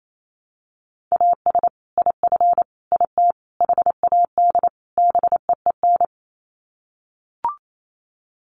Morse code, heard as a keyed tone, about 28 wpm, 700 Hz. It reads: AS IF IT HAD BEEN
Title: Sets of 5 Words Encore2 28wpm